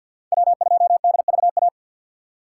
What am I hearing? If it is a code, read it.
Morse code, heard as a keyed tone, about 48 wpm, 700 Hz. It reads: W2DVA
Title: USA Call Signs - 5 Characters - 6x Character Spacing - 50wpm